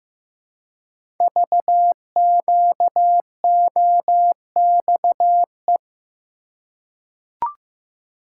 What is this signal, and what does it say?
Morse code, heard as a keyed tone, about 15 wpm, 700 Hz. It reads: VQOXE
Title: Code Groups - 5 Letters - 15wpm